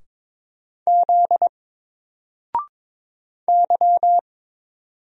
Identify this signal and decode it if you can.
Morse code, heard as a keyed tone, about 22 wpm, 700 Hz. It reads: Z Y
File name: Single-Letter-Number-22wpm